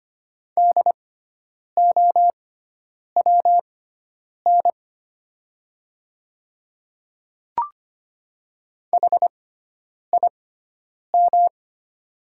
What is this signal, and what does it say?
Morse code, heard as a keyed tone, about 25 wpm, spaced as for 9 wpm, 700 Hz. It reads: DOWN HIM
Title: 6x Character Spacing - Top 500 Words - 25wpm